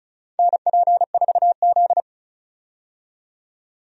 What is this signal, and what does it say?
Morse code, heard as a keyed tone, about 34 wpm, 700 Hz. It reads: NP4Z
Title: USA Call Signs - 4 Characters - 35wpm